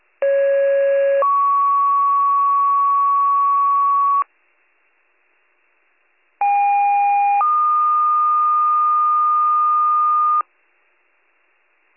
Two-tone page, heard as a one-second tone, then a three-second tone, about 550 then 1120 Hz, about 810 then 1220 Hz. It is heard from 0.2 to 10.4 s.